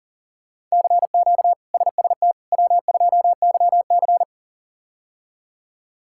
Morse code, heard as a keyed tone, about 40 wpm, 700 Hz. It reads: CQ SST W2YC
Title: K1USN SST QSO Practice - 40wpm